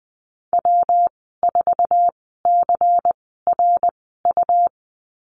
Morse code, heard as a keyed tone, about 20 wpm, spaced as for 14 wpm, 700 Hz. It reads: W4CRU
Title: 154 Lesson 25g - Letter p - Call Signs 20wpm